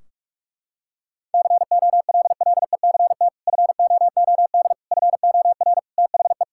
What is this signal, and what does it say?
Morse code, heard as a keyed tone, about 44 wpm, 700 Hz. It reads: COLLECT FOOD FOR THE
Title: Sentences from Top 1600 Words 45wpm